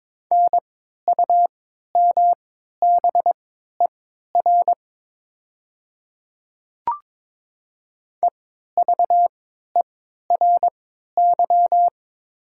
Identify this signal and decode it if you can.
Morse code, heard as a keyed tone, about 22 wpm, spaced as for 13 wpm, 700 Hz. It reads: NUMBER EVERY
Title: 3x Character Spacing - Top 200 Words - 22wpm